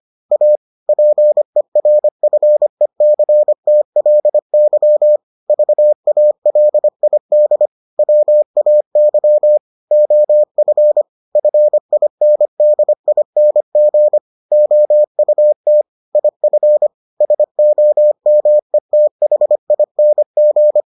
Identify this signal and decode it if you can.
Morse code, heard as a keyed tone, about 25 wpm, 600 Hz. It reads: A PERFECTLY VALID WAY OF FINDING OUT IF SOMETHING